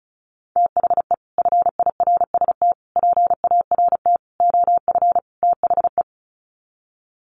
Morse code, heard as a keyed tone, about 34 wpm, 700 Hz. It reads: THE FIRST PART OF THE
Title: Sets of 5 Words Encore 35wpm